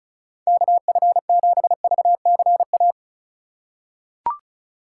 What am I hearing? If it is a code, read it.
Morse code, heard as a keyed tone, about 36 wpm, 700 Hz. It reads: KF7VCA
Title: USA Call Signs - 6 Characters - 4x Character Spacing - 35wpm